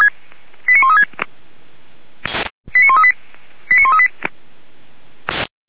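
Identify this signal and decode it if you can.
Five-tone selective call (ZVEI), heard as quick runs of five tones, tones of about 70 ms. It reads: 89136 89138 89138
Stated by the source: ZVEI five-tone call 89135 received by SDR, archived by audio.f85360.feuerwehr-orlamuende.org fire brigade.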